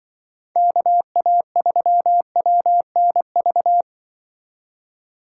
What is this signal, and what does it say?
Morse code, heard as a keyed tone, about 24 wpm, 700 Hz. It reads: KA3WNV
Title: Speed Racing - USA 2x3 Call Signs 20wpm